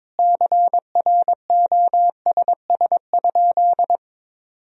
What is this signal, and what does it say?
Morse code, heard as a keyed tone, about 22 wpm, 700 Hz. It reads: CROSS?
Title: Sentences from Top 300 Words and Abbreviations 22wpm